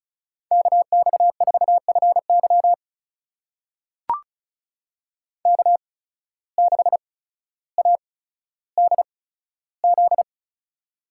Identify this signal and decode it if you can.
Morse code, heard as a keyed tone, about 34 wpm, spaced as for 15 wpm, 700 Hz. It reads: KX4FY K6ADZ
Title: USA Call Signs - 8x Character Spacing - 35wpm